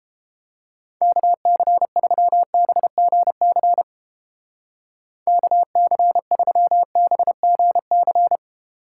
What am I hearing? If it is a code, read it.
Morse code, heard as a keyed tone, about 29 wpm, 700 Hz. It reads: KC3BGC KC3BGC